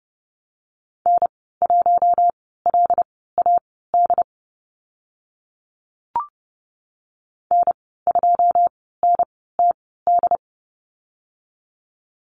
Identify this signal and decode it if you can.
Morse code, heard as a keyed tone, about 30 wpm, spaced as for 17 wpm, 700 Hz. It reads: N1LAD N2NTD
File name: Lesson 14g - Number 2 - Call Signs 30wpm